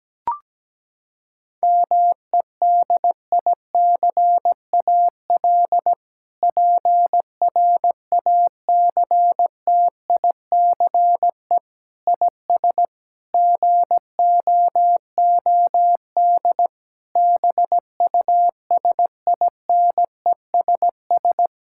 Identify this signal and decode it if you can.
Morse code, heard as a keyed tone, about 17 wpm, 700 Hz. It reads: MEDICAL PRACTICE IS GOOD BUSINESS
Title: Sentences from Top 1000 Words - Review Part 5 - 17wpm